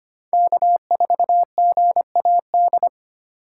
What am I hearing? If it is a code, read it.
Morse code, heard as a keyed tone, about 25 wpm, 700 Hz. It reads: K4GAD
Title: USA Call Signs - 5 Characters - 4x Character Spacing - 25wpm